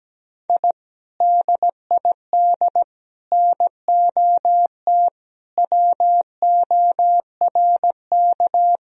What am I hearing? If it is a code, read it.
Morse code, heard as a keyed tone, about 17 wpm, 700 Hz. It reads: I DID NOT WORK
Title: Sentences from Top 1000 Words - Review Part 2 - 17wpm